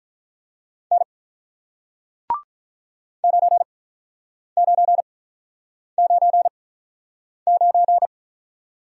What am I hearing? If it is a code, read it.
Morse code, heard as a keyed tone, about 45 wpm, spaced as for 9 wpm, 700 Hz. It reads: N 9999